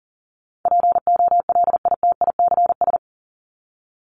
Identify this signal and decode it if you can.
Morse code, heard as a keyed tone, about 40 wpm, 700 Hz. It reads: POLITICS
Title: Top 751-1000 Words 40wpm